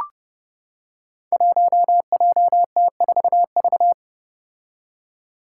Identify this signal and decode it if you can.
Morse code, heard as a keyed tone, about 30 wpm, 700 Hz. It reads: 1JT4V
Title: Five Challenging Letter-Number Combinations 30wpm